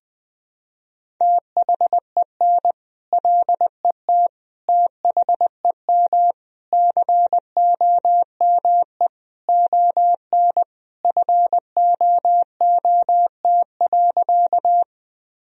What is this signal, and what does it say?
Morse code, heard as a keyed tone, about 20 wpm, 700 Hz. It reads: THEN LET THEM COME ON FOOT.
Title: Sentences from Top 1000 Words - Review Part 3 - 20wpm